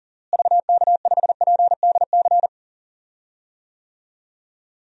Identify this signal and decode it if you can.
Morse code, heard as a keyed tone, about 40 wpm, 700 Hz. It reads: VK5PDC
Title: DXCC-100 Call Signs - 6 Characters - Repeated 2x - 40wpm